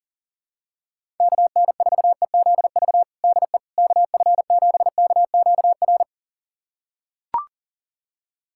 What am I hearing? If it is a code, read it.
Morse code, heard as a keyed tone, about 37 wpm, 700 Hz. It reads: KN4EZV DE KF7KQR
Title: USA Callsign DE Callsign 40wpm